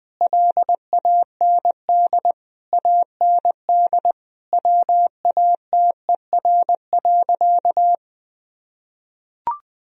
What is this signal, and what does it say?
Morse code, heard as a keyed tone, about 20 wpm, 700 Hz. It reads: LAND AND WATER.
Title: Sentences from Top 1000 Words - Review Part 1 - 20wpm